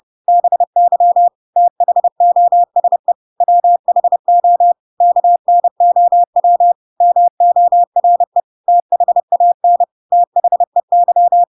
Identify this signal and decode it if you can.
Morse code, heard as a keyed tone, about 30 wpm, 700 Hz. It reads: BY THOSE WHO KNOW MORE THAN THEY